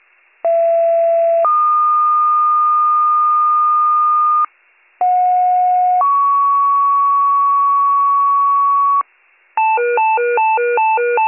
Two-tone page, heard as a one-second tone, then a three-second tone, about 670 then 1220 Hz, about 730 then 1120 Hz. It is heard from 0.4 to 9.0 s.